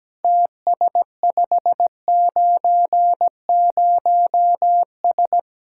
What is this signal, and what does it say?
Morse code, heard as a keyed tone, about 17 wpm, 700 Hz. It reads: TS590S
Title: Speed-Racing Common QSO Phrases Encore - 17wpm